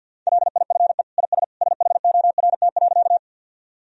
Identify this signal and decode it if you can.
Morse code, heard as a keyed tone, about 48 wpm, 700 Hz. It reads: LIFE IS SHORT.
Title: Sentences from Top 1000 Words - Review Part 2 - 50wpm